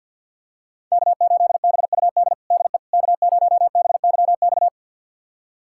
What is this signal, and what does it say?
Morse code, heard as a keyed tone, about 46 wpm, 700 Hz. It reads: K8BUD DE K0BYX
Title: USA Callsign DE Callsign - 5 Characters - 50wpm